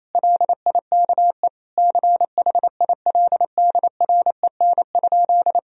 Morse code, heard as a keyed tone, about 27 wpm, 700 Hz. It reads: LIKE CHILDREN?
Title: Sentences from Top 1000 Words - Review Part 2 - 28wpm